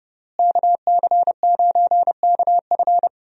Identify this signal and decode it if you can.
Morse code, heard as a keyed tone, about 32 wpm, 700 Hz. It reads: KC9KF
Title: USA Call Signs - 5 Characters - Repeated 2x - 30wpm